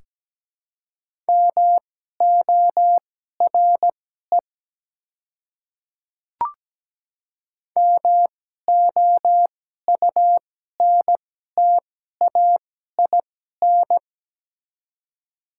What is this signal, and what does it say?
Morse code, heard as a keyed tone, about 17 wpm, spaced as for 12 wpm, 700 Hz. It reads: MORE MOUNTAIN
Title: 2x Character Spacing - Top 300 Words - 17wpm